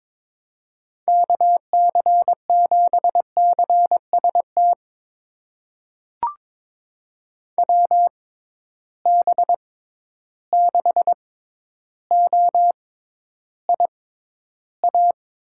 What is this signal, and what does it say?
Morse code, heard as a keyed tone, about 22 wpm, spaced as for 11 wpm, 700 Hz. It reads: KC7CST WB6OIA
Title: USA Call Signs - 6 Characters - 6x Character Spacing - 22wpm